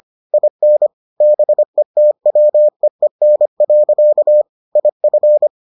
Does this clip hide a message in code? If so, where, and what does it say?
Yes: Morse code from 0.3 to 5.5 s, heard as a keyed tone, about 25 wpm, 600 Hz. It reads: IN BETWEEN. IF